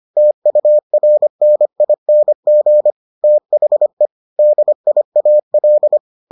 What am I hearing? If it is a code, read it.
Morse code, heard as a keyed tone, about 25 wpm, 600 Hz. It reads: TURNING THE DIAL